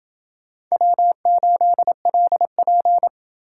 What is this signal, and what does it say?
Morse code, heard as a keyed tone, about 28 wpm, 700 Hz. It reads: W8LP